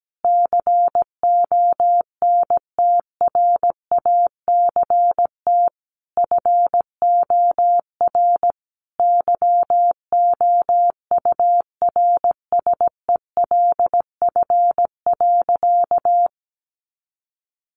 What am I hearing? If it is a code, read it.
Morse code, heard as a keyed tone, about 17 wpm, 700 Hz. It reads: CONTRACT FOR YOURSELF.
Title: Sentences from Top 1400 Words 17wpm